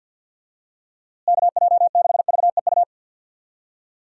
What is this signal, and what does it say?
Morse code, heard as a keyed tone, about 52 wpm, 700 Hz. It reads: KJ6VEU
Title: USA Call Signs - 6 Characters - Repeated 3x - 50wpm